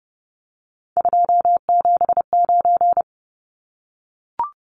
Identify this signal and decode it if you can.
Morse code, heard as a keyed tone, about 30 wpm, 700 Hz. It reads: 279
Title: Numbers - 3-Digit 30wpm